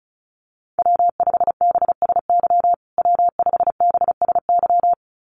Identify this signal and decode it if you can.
Morse code, heard as a keyed tone, about 34 wpm, 700 Hz. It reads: W5BSY W5BSY
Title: USA Call Signs Repeated 2x 35wpm